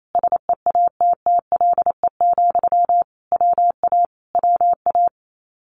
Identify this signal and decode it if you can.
Morse code, heard as a keyed tone, about 28 wpm, 700 Hz. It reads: SEATTLE, WA WA
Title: Common QSO Phrases Encore 28wpm